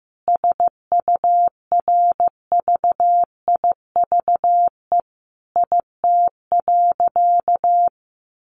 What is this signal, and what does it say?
Morse code, heard as a keyed tone, about 15 wpm, 700 Hz. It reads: SURVIVE IT.
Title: Sentences from Top 1300 Words 15wpm